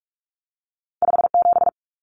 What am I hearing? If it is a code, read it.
Morse code, heard as a keyed tone, about 46 wpm, 700 Hz. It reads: 57
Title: Common Cut Numbers - 2-Digit 45wpm